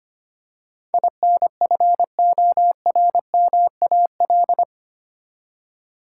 Morse code, heard as a keyed tone, about 25 wpm, 700 Hz. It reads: INFORMAL
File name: Eight Letter Words - 25wpm